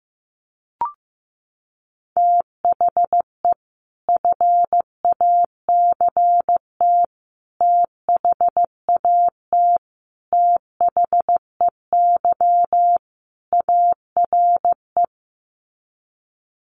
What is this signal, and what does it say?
Morse code, heard as a keyed tone, about 15 wpm, 700 Hz. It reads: THE FACT THAT THEY ARE
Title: Sets of 5 Words Encore2 15wpm